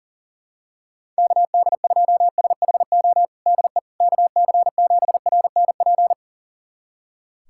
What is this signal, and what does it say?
Morse code, heard as a keyed tone, about 40 wpm, 700 Hz. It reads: KD2SHO DE KC7RNP
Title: USA Callsign DE Callsign - 6 Characters - 40wpm